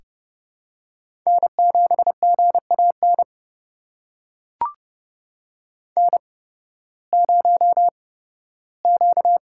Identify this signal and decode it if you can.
Morse code, heard as a keyed tone, about 31 wpm, spaced as for 16 wpm, 700 Hz. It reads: N7GAN N0Q
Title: USA Call Signs - 5 Characters - 8x Character Spacing - 30wpm